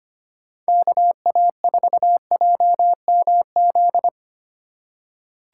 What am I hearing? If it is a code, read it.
Morse code, heard as a keyed tone, about 25 wpm, 700 Hz. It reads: KA4JMZ
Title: USA Call Signs - 6 Characters - 4x Character Spacing - 25wpm